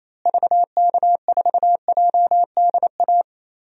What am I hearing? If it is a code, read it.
Morse code, heard as a keyed tone, about 28 wpm, 700 Hz. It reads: VK4JDA
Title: DXCC-100 Call Signs - 6 Characters - Repeated 3x - 28wpm